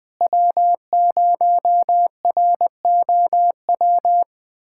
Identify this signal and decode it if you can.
Morse code, heard as a keyed tone, about 20 wpm, 700 Hz. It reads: W0ROW
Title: USA Callsign DE Callsign - 5 Characters - 20wpm